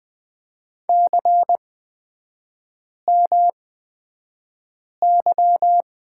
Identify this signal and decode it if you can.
Morse code, heard as a keyed tone, about 20 wpm, spaced as for 5 wpm, 700 Hz. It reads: CMY
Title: 236 Lesson 38e - Letter q - Three Characters 20wpm